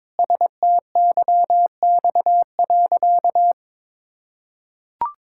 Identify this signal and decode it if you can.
Morse code, heard as a keyed tone, about 22 wpm, 700 Hz. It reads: STYX.
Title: Pangram Challenge v2 - 22wpm